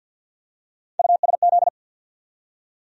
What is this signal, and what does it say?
Morse code, heard as a keyed tone, about 53 wpm, 700 Hz. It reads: USZ
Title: USA Call Sign Postfixes - 3 Letters 50wpm